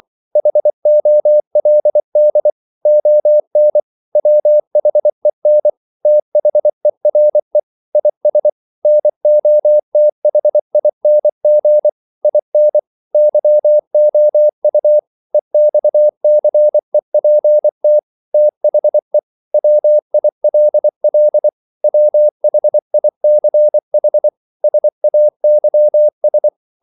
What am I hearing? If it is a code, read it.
Morse code, heard as a keyed tone, about 24 wpm, 600 Hz. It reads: HOLD ON WHEN THERE IS NOTHING IN YOU EXCEPT THE WILL WHICH SAYS
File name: CW / Practice / If24